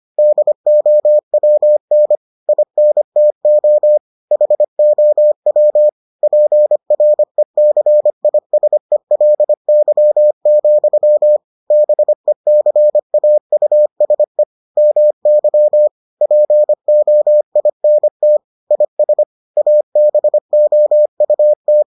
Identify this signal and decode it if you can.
Morse code, heard as a keyed tone, about 25 wpm, 600 Hz. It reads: DOWN INTO HOW PRECISELY, BECAUSE MY POINT IS ABOUT